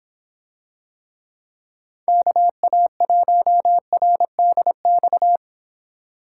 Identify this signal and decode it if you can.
Morse code, heard as a keyed tone, about 26 wpm, 700 Hz. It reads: KA1RDX